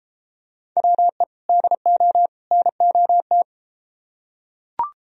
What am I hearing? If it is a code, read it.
Morse code, heard as a keyed tone, about 33 wpm, 700 Hz. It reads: WE DO NOT